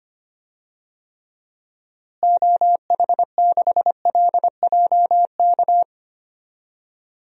Morse code, heard as a keyed tone, about 25 wpm, 700 Hz. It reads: OH6LJK
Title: DXCC-100 Call Signs - 6 Characters - Repeated 2x - 25wpm